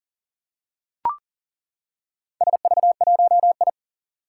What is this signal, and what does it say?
Morse code, heard as a keyed tone, about 38 wpm, 700 Hz. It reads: SV1I
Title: Four Challenging Letter-Number Combinations 40wpm